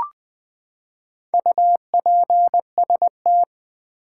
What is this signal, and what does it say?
Morse code, heard as a keyed tone, about 20 wpm, 700 Hz. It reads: UPST
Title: Build A Word - 4 Letters At A Time - 8-Letter Words - 20wpm